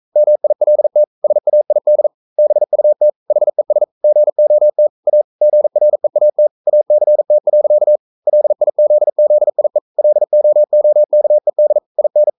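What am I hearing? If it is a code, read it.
Morse code, heard as a keyed tone, about 42 wpm, 600 Hz. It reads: MILT SAID BUT HES GOT A GREAT ACT. LIZZIE LOOKED IN
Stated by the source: Morse code practice at 41 wpm, from CW med Gnister.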